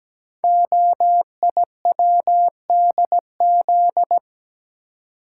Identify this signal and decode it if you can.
Morse code, heard as a keyed tone, about 17 wpm, 700 Hz. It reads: OIWDZ
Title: Code Groups - 5 Letters - 17wpm